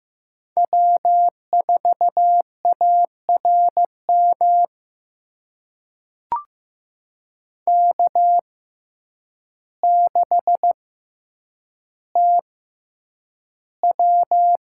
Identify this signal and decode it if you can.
Morse code, heard as a keyed tone, about 15 wpm, spaced as for 8 wpm, 700 Hz. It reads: W4ARM K6TW
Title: USA Call Signs - 5 Characters - 6x Character Spacing - 15wpm